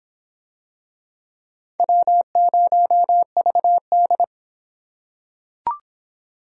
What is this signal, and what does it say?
Morse code, heard as a keyed tone, about 25 wpm, 700 Hz. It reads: W0VD